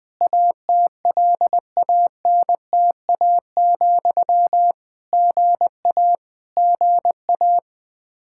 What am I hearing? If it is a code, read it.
Morse code, heard as a keyed tone, about 21 wpm, 700 Hz. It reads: ATLANTA, GA GA
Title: K1USN SST QSO Practice 20wpm